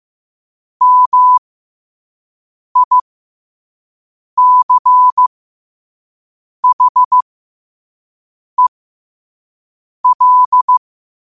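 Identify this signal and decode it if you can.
Morse code, heard as a keyed tone, about 15 wpm, spaced as for 5 wpm, 1000 Hz. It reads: MICHEL
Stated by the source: Morse code quote of the day sent at 5 WPM.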